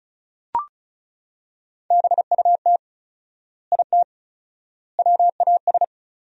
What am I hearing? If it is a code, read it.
Morse code, heard as a keyed tone, about 36 wpm, 700 Hz. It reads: BUT IT WAS
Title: 4x Word-Spacing - Sentences from Top 100 Words - 35wpm